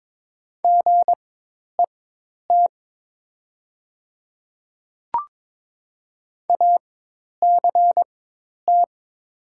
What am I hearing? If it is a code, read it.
Morse code, heard as a keyed tone, about 23 wpm, spaced as for 10 wpm, 700 Hz. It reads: GET ACT
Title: 4x Character Spacing - Top 500 Words - 22wpm